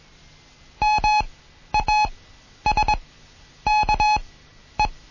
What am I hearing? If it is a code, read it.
Morse code, heard as a keyed tone, about 21 wpm, spaced as for 11 wpm, 850 Hz. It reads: MASKE